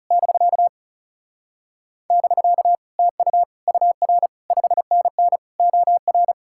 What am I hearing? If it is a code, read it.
Morse code, heard as a keyed tone, about 40 wpm, 700 Hz. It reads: <BK> <BK> TU UR 5NN OR